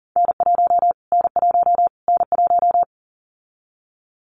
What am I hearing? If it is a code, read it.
Morse code, heard as a keyed tone, about 40 wpm, 700 Hz. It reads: N1 N1 N1
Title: Callsign Prefixes for Top 25 DXCC - Repeated 3x - 40wpm